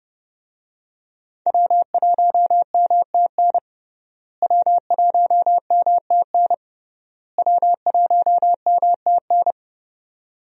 Morse code, heard as a keyed tone, about 30 wpm, 700 Hz. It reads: W1MTN W1MTN W1MTN
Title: USA Call Signs - 5 Characters - Repeated 3x - 30wpm